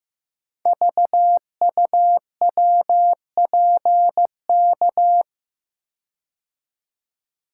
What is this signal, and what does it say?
Morse code, heard as a keyed tone, about 15 wpm, 700 Hz. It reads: VUWPK
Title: Code Groups - 5 Letters - 15wpm